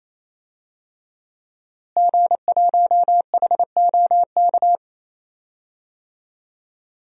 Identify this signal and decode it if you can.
Morse code, heard as a keyed tone, about 28 wpm, 700 Hz. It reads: G1HOK
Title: DXCC-100 Call Signs - 5 Characters - Repeated 3x - 28wpm